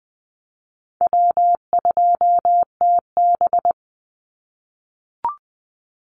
Morse code, heard as a keyed tone, about 20 wpm, 700 Hz. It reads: W2TB
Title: Call Signs - 20wpm